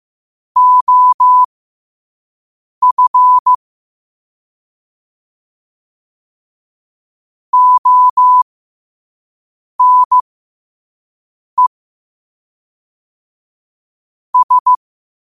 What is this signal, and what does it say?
Morse code, heard as a keyed tone, about 15 wpm, spaced as for 5 wpm, 1000 Hz. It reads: OF ONE S
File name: qotdfiles / 5wpm